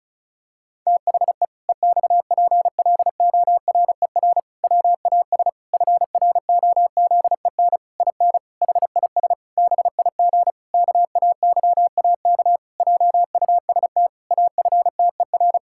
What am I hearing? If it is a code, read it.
Morse code, heard as a keyed tone, about 36 wpm, 700 Hz. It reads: THE EXPLORER WAS FROZEN IN HIS BIG KAYAK JUST AFTER
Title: Pangram Challenge v2 - 35wpm